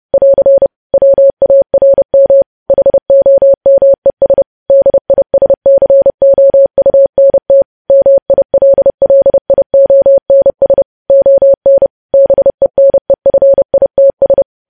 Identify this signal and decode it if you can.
Morse code, heard as a keyed tone, about 30 wpm, 550 Hz. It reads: <AR> WARM HOMES DISCOUNT MILLIONS ON BENEFITS
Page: news headlines in morse code at 30 wpm